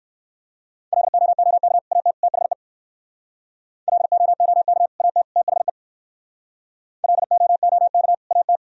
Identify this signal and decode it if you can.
Morse code, heard as a keyed tone, about 58 wpm, 700 Hz. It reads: LOOK AT THE LOOK AT THE LOOK AT